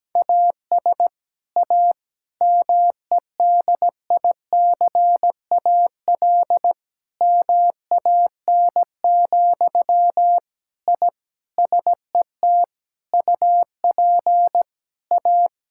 Morse code, heard as a keyed tone, about 17 wpm, 700 Hz. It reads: AS A MEDICAL MAN, I SET UP A
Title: Sentences from Top 1000 Words - Review Part 5 - 17wpm